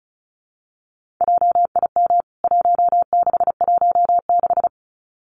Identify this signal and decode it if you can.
Morse code, heard as a keyed tone, about 34 wpm, 700 Hz. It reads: JIM 1616
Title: CWT Exchange 35wpm